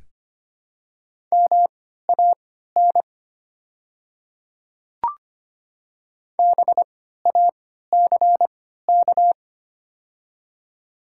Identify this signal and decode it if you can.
Morse code, heard as a keyed tone, about 25 wpm, spaced as for 14 wpm, 700 Hz. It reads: MAN BACK
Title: 3x Character Spacing - Top 200 Words - 25wpm